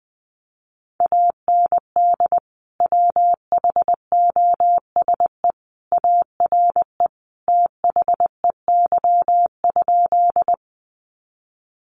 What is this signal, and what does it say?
Morse code, heard as a keyed tone, about 20 wpm, 700 Hz. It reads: AND WHOSE ARE THEY?
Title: Sentences from Top 1200 Words 20wpm